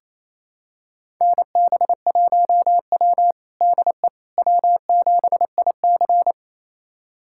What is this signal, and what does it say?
Morse code, heard as a keyed tone, about 28 wpm, 700 Hz. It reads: NB1W DE W7IC
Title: USA Callsign DE Callsign - 4 Characters - 28wpm